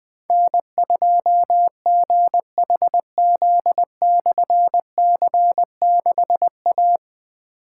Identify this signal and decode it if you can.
Morse code, heard as a keyed tone, about 20 wpm, 700 Hz. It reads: N2GHZ/C6A
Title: Difficult Call Signs 20wpm